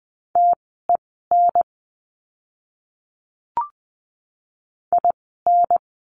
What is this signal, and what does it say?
Morse code, heard as a keyed tone, about 20 wpm, spaced as for 14 wpm, 700 Hz. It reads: TEN IN
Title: 030 Lesson 6f - Letter i - Words 20wpm